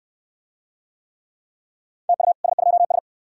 Morse code, heard as a keyed tone, about 69 wpm, 700 Hz. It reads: TU S2S